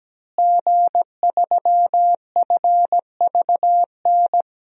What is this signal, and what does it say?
Morse code, heard as a keyed tone, about 17 wpm, 700 Hz. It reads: G3FVN
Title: DXCC-100 Call Signs - 5 Characters - Repeated 3x - 17wpm